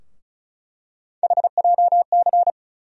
Morse code, heard as a keyed tone, about 36 wpm, 700 Hz. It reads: HJC
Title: USA Call Sign Postfixes - 3 Letters 35wpm